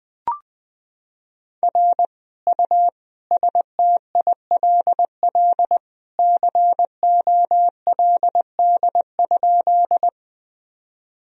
Morse code, heard as a keyed tone, about 20 wpm, 700 Hz. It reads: R U STILL COLD?
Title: Sentences from Top 400 Words and Abbreviations 20wpm